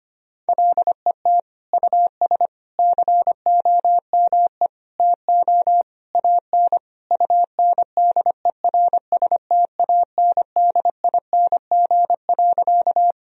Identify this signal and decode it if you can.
Morse code, heard as a keyed tone, about 25 wpm, 700 Hz. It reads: LET US COME TO AN UNDERSTANDING.
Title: Sentences from Top 1000 Words - Review Part 3 - 25wpm